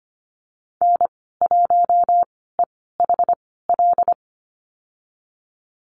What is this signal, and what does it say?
Morse code, heard as a keyed tone, about 25 wpm, spaced as for 16 wpm, 700 Hz. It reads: N1EHL
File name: Lesson 15g - Number 5 - Call Signs 25wpm